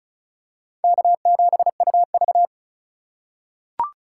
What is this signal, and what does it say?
Morse code, heard as a keyed tone, about 36 wpm, 700 Hz. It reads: K7UV
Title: USA Call Signs - 4 Characters - 2x Character Spacing - 35wpm